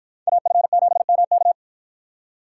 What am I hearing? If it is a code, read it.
Morse code, heard as a keyed tone, about 53 wpm, 700 Hz. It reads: AF7MK